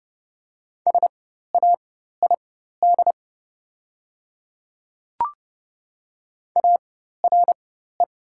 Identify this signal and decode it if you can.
Morse code, heard as a keyed tone, about 30 wpm, spaced as for 14 wpm, 700 Hz. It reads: SAID ARE
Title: 4x Character Spacing - Top 100 Words - 30wpm